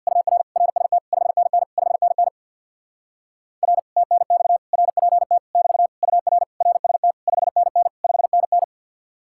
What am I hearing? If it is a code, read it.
Morse code, heard as a keyed tone, about 64 wpm, 700 Hz. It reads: UR RST 5NN 5NN R TNX RPT <BT> UR RST 5NN 5NN